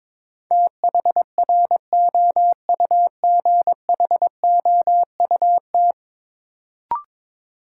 Morse code, heard as a keyed tone, about 22 wpm, 700 Hz. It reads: THROUGHOUT
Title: Sentences from Top 800 Words - New Word Drill - 22wpm